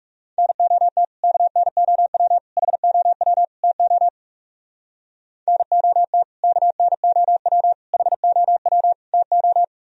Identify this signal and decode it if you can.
Morse code, heard as a keyed tone, about 39 wpm, 700 Hz. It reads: NOT KNOW HOW TO NOT KNOW HOW TO